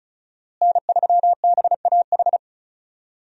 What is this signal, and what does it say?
Morse code, heard as a keyed tone, about 34 wpm, 700 Hz. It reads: N3BAH
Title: USA Call Signs - 5 Characters - Repeated 2x - 35wpm